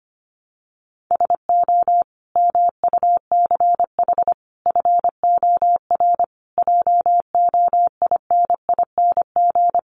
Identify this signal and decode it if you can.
Morse code, heard as a keyed tone, about 25 wpm, 700 Hz. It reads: SO MUCH FOR JOINING
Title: Sets of 5 Words 25wpm